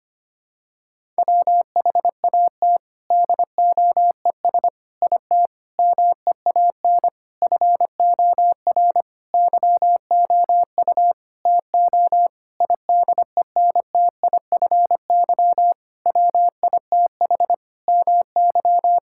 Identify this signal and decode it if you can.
Morse code, heard as a keyed tone, about 25 wpm, 700 Hz. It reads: WHAT DOES IT MEAN FOR YOU TO IDENTIFY WITH MY
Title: Sentences from Top 1000 Words - Review Part 6 - 25wpm